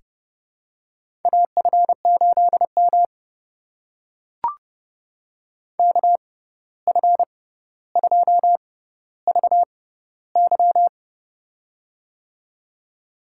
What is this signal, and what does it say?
Morse code, heard as a keyed tone, about 30 wpm, spaced as for 14 wpm, 700 Hz. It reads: AF8M KF2VY
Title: USA Call Signs - 6x Character Spacing - 30wpm